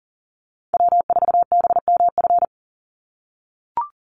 Morse code, heard as a keyed tone, about 40 wpm, 700 Hz. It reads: W4BMF
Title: USA Call Signs 40wpm